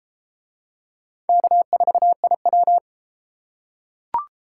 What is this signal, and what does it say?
Morse code, heard as a keyed tone, about 31 wpm, 700 Hz. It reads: K4IW